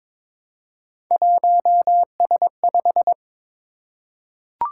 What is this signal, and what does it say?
Morse code, heard as a keyed tone, about 23 wpm, 700 Hz. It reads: 1S5
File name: practice / 22wpm / Three Letter-Number Combinations - 22wpm